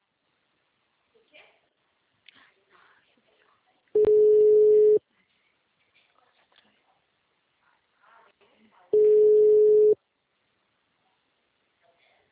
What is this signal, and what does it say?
Ringback tone (Europe, 425 Hz), with speech.